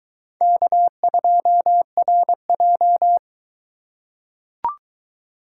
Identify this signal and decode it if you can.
Morse code, heard as a keyed tone, about 22 wpm, 700 Hz. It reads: K2RJ